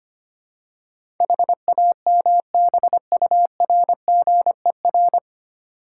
Morse code, heard as a keyed tone, about 25 wpm, 690 Hz. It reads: HAMBURGER
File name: practice / 25wpm / Easy Words with Hints - 25wpm